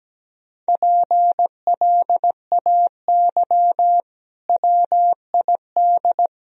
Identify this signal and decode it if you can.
Morse code, heard as a keyed tone, about 17 wpm, 700 Hz. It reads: PLAY WID
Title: Sentences from Top 200 Words and Abbreviations 17wpm